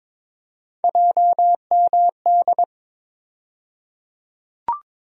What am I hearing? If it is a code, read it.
Morse code, heard as a keyed tone, about 22 wpm, 700 Hz. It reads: JMD
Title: USA Call Sign Postfixes - 3 Letters 22wpm